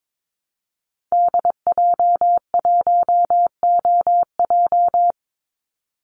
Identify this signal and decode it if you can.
Morse code, heard as a keyed tone, about 22 wpm, 700 Hz. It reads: DJ1OJ
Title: Call Signs Repeated 2x - 22wpm